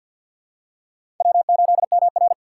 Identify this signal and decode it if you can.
Morse code, heard as a keyed tone, about 47 wpm, 700 Hz. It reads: W8MR
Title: USA Call Signs - 4 Characters - 6x Character Spacing - 50wpm